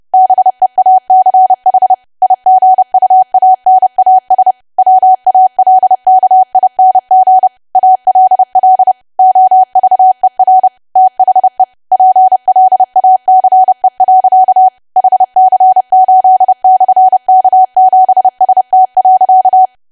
Morse code, heard as a keyed tone, about 30 wpm, 750 Hz. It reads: BEACH IGUANAS WALKING ALL OVER THE PLACE. HC8/K7ST.
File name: cwa / W1AW-Level3 / 30WPM